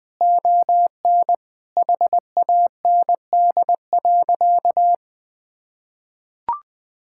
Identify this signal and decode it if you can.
Morse code, heard as a keyed tone, about 20 wpm, 700 Hz. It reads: ON HAND.
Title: Sentences from Top 1000 Words - Review Part 5 - 20wpm